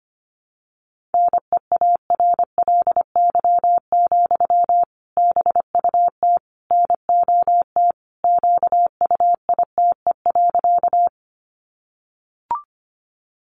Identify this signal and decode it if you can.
Morse code, heard as a keyed tone, about 25 wpm, 700 Hz. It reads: NEARLY, BUT NOT QUITE.
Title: Sentences from Top 1300 Words 25wpm